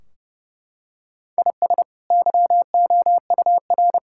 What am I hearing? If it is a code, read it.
Morse code, heard as a keyed tone, about 30 wpm, 700 Hz. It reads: IS YOUR
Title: Sentences from Top 1600 Words 30wpm